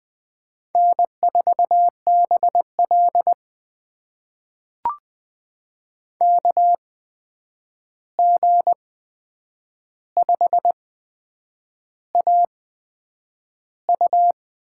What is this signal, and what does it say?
Morse code, heard as a keyed tone, about 20 wpm, spaced as for 8 wpm, 700 Hz. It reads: N4BL KG5AU
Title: USA Call Signs - 8x Character Spacing - 20wpm